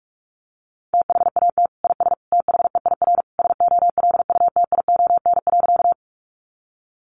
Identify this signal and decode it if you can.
Morse code, heard as a keyed tone, about 44 wpm, 700 Hz. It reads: THAT IS THEIR SOLUTION.
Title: Sentences from Top 1300 Words 45wpm